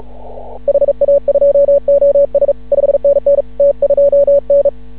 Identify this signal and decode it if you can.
Morse code, heard as a keyed tone, about 36 wpm, 560 Hz. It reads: HA2OS 5NN T2N
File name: CW / ha6oi 2